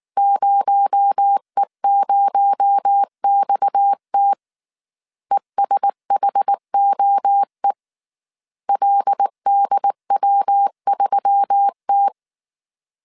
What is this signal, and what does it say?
Morse code, heard as a keyed tone, about 19 wpm, 800 Hz. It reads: 0E0XT ESHOE LDW3T
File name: CW / DET / 1 6 1500C S A 18wpm 2255